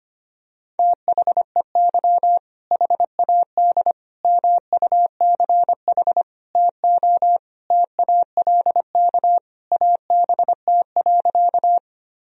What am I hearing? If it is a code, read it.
Morse code, heard as a keyed tone, about 25 wpm, 700 Hz. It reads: THEY HAD MUCH TO TALK ABT.
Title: Sentences from Top 400 Words and Abbreviations 25wpm